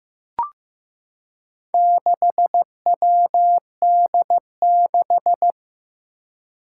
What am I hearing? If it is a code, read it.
Morse code, heard as a keyed tone, about 15 wpm, 700 Hz. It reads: 6WD6
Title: Four Challenging Letter-Number Combinations 15wpm